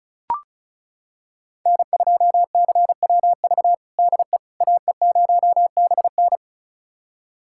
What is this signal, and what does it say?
Morse code, heard as a keyed tone, about 34 wpm, 700 Hz. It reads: N2CWV DE AE0BN
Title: USA Callsign DE Callsign - 5 Characters - 35wpm